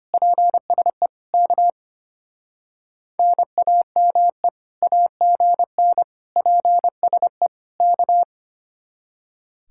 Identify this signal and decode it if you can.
Morse code, heard as a keyed tone, about 32 wpm, 700 Hz. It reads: PSE K NAME AGN PSE K